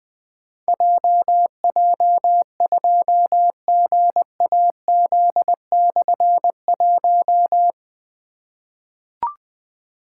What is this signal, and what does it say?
Morse code, heard as a keyed tone, about 20 wpm, 700 Hz. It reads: JJ2GAZ/1
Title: Difficult Call Signs 20wpm